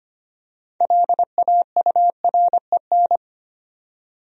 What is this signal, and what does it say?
Morse code, heard as a keyed tone, about 25 wpm, 700 Hz. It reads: LAUREN
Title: USA Common Names - 25wpm